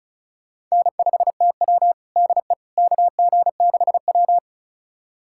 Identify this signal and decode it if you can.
Morse code, heard as a keyed tone, about 36 wpm, 700 Hz. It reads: N5TW DE KG6W
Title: USA Callsign DE Callsign - 4 Characters - 35wpm